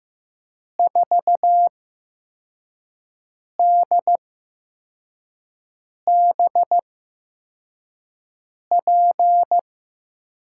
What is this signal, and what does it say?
Morse code, heard as a keyed tone, about 15 wpm, spaced as for 4 wpm, 700 Hz. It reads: 4DBP
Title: USA Call Signs - 6 Characters - 8x Character Spacing - 15wpm